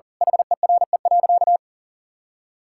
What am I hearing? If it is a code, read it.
Morse code, heard as a keyed tone, about 40 wpm, 700 Hz. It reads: HERE.
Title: Sentences from Top 1000 Words - Review Part 6 - 40wpm